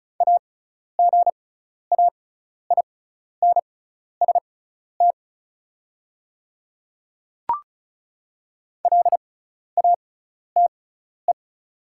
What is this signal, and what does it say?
Morse code, heard as a keyed tone, about 36 wpm, spaced as for 12 wpm, 700 Hz. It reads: AGAINST LATE
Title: 6x Character Spacing - Top 500 Words - 35wpm